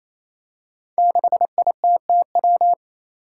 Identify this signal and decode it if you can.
Morse code, heard as a keyed tone, about 28 wpm, 700 Hz. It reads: 6ITTW
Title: Five Challenging Letter-Number Combinations 28wpm